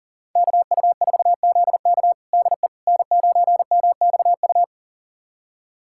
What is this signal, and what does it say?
Morse code, heard as a keyed tone, about 40 wpm, 690 Hz. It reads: KU4ZK DE N9MXU
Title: USA Callsign DE Callsign - 5 Characters - 40wpm